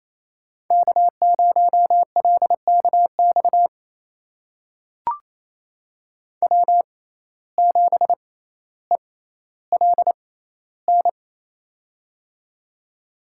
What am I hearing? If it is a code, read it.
Morse code, heard as a keyed tone, about 28 wpm, spaced as for 14 wpm, 700 Hz. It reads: K0LKX W7ELN
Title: USA Call Signs - 5 Characters - 6x Character Spacing - 28wpm